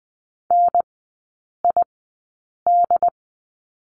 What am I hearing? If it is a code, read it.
Morse code, heard as a keyed tone, about 20 wpm, spaced as for 8 wpm, 700 Hz. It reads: NID
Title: 074 Lesson 13e - Letter L - Three Characters 20wpm